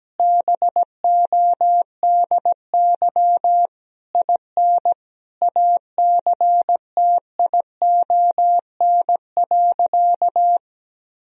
Morse code, heard as a keyed tone, about 17 wpm, 700 Hz. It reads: BODY IN ACTION.
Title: Sentences from Top 1000 Words - Review Part 4 - 17wpm